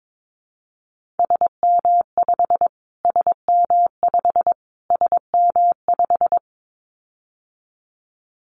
Morse code, heard as a keyed tone, about 22 wpm, 700 Hz. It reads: SM5 SM5 SM5
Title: Callsign Prefixes for Top 25 DXCC - Repeated 3x - 22wpm